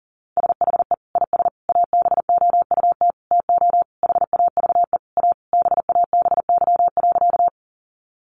Morse code, heard as a keyed tone, about 40 wpm, 700 Hz. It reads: SHE IS ABOUT TO HAVE A BABY.
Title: Sentences from Top 1300 Words 40wpm